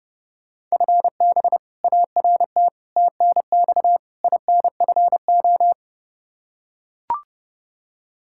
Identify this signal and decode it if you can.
Morse code, heard as a keyed tone, about 32 wpm, 700 Hz. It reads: FB ART TNX INFO
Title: K1USN SST QSO Practice 30wpm